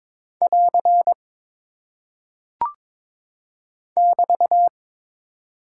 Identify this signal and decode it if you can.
Morse code, heard as a keyed tone, about 22 wpm, 700 Hz. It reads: <AR> <BT>
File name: practice / 22wpm / Single Punctuation-Prosign - 22wpm